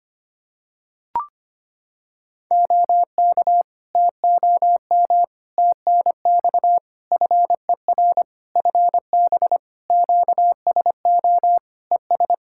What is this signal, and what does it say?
Morse code, heard as a keyed tone, about 25 wpm, 700 Hz. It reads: OK TOM TNX FER FB QSO ES
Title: Practice QSO - Exchange 4 - Ending - 25wpm